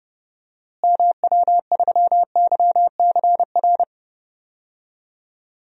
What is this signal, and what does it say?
Morse code, heard as a keyed tone, about 30 wpm, 700 Hz. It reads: MW3YCR
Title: DXCC-100 Call Signs - 6 Characters - Repeated 3x - 30wpm